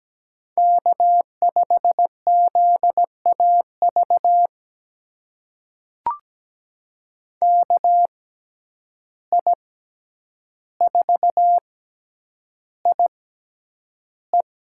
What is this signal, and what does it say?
Morse code, heard as a keyed tone, about 17 wpm, spaced as for 9 wpm, 700 Hz. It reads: K5ZAV KI4IE
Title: USA Call Signs - 5 Characters - 6x Character Spacing - 17wpm